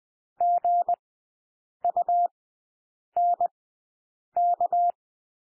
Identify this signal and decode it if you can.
Morse code, heard as a keyed tone, about 20 wpm, spaced as for 8 wpm, 700 Hz. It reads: GUNK
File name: cwmp3 / W1 SOUND WORDS 20 wpm